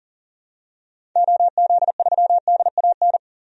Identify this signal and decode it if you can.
Morse code, heard as a keyed tone, about 40 wpm, 700 Hz. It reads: OZ3DAN
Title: DXCC-100 Call Signs - 6 Characters - 40wpm